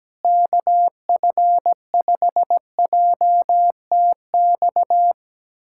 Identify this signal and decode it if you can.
Morse code, heard as a keyed tone, about 17 wpm, 700 Hz. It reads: KF5JTX
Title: USA Callsign DE Callsign - 6 Characters - 17wpm